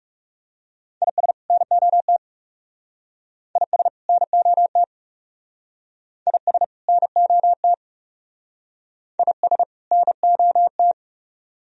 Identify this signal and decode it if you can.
Morse code, heard as a keyed tone, about 38 wpm, 700 Hz. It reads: IS NOT IS NOT IS NOT IS NOT